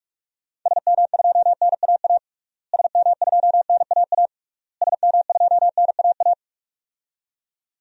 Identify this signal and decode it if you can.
Morse code, heard as a keyed tone, about 46 wpm, 700 Hz. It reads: SM2NAA SM2NAA SM2NAA
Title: DXCC-100 Call Signs - 6 Characters - Repeated 3x - 45wpm